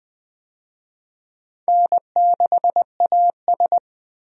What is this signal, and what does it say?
Morse code, heard as a keyed tone, about 20 wpm, 700 Hz. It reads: N6AS
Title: USA Call Signs - 4 Characters - 20wpm